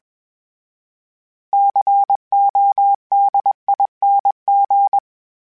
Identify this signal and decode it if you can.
Morse code, heard as a keyed tone, about 21 wpm, 800 Hz. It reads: CODING
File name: Special INGs / Special INGs-mp3s / Sngl Ing @20